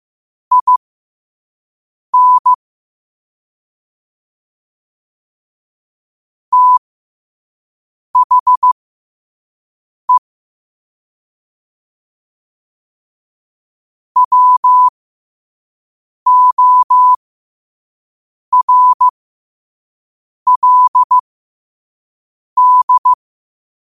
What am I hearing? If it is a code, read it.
Morse code, heard as a keyed tone, about 15 wpm, spaced as for 5 wpm, 1000 Hz. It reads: IN THE WORLD
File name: qotdfiles / 5wpm